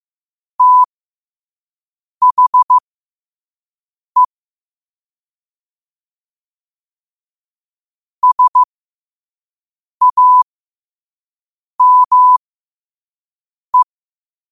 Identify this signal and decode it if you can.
Morse code, heard as a keyed tone, about 15 wpm, spaced as for 5 wpm, 1000 Hz. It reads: THE SAME
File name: qotdfiles / 5wpm